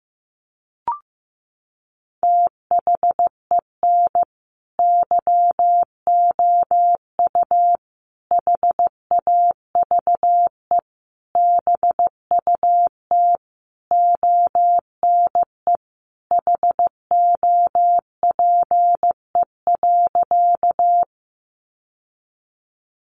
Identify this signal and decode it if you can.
Morse code, heard as a keyed tone, about 15 wpm, 700 Hz. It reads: THEN YOU HAVE BUT ONE HOPE.
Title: Sentences from Top 1000 Words 15wpm